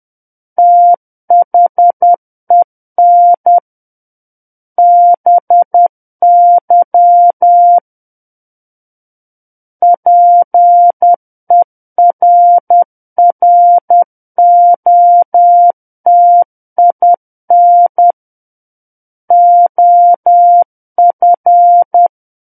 Morse code, heard as a keyed tone, about 10 wpm, 700 Hz. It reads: THEN BY PERROTIN OF
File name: Book One / 01-Chapter 1 - 10 WPM